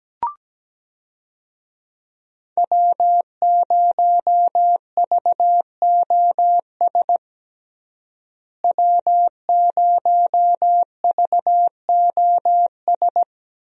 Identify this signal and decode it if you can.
Morse code, heard as a keyed tone, about 17 wpm, 700 Hz. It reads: W0VOS W0VOS
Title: USA Call Signs - 5 Characters - Repeated 3x - 17wpm